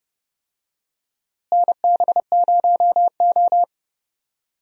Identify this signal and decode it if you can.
Morse code, heard as a keyed tone, about 30 wpm, 700 Hz. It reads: NB0O
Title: USA Call Signs - 4 Characters - Repeated 2x - 30wpm